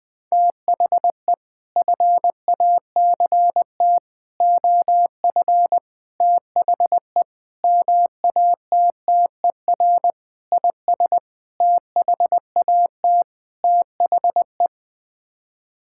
Morse code, heard as a keyed tone, about 20 wpm, 700 Hz. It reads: THE FACT OF THE MATTER IS THAT THE
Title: Sets of 8 Words - 20wpm